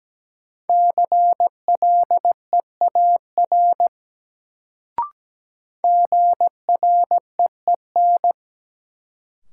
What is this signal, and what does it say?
Morse code, heard as a keyed tone, about 17 wpm, 700 Hz. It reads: CLEAR GREEN
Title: Sentences from Top 500 Words - New Word Drill - 17wpm